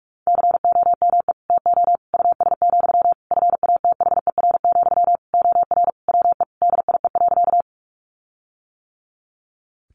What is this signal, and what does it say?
Morse code, heard as a keyed tone, about 44 wpm, 700 Hz. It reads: COME TO US, FATHER, OR WE DIE.
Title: Sentences from Top 1300 Words 45wpm